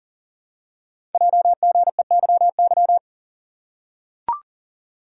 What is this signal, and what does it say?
Morse code, heard as a keyed tone, about 38 wpm, 690 Hz. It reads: JGEYY
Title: Code Groups - 5 Letters - 40wpm